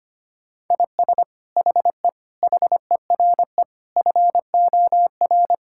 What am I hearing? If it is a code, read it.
Morse code, heard as a keyed tone, about 25 wpm, 700 Hz. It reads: IS HE HERE FOR
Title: Sentences from Top 1600 Words 25wpm